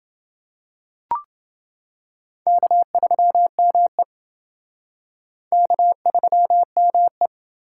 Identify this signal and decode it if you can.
Morse code, heard as a keyed tone, about 30 wpm, 700 Hz. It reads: K3ME K3ME